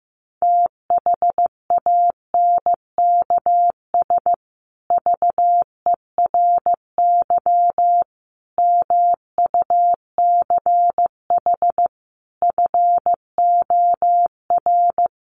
Morse code, heard as a keyed tone, about 15 wpm, 700 Hz. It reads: THANKS VERY MUCH FOR
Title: Sets of 5 Words Encore2 15wpm